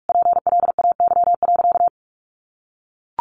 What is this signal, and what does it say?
Morse code, heard as a keyed tone, about 44 wpm, 700 Hz. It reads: PLAY.
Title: Easy Phrases 45wpm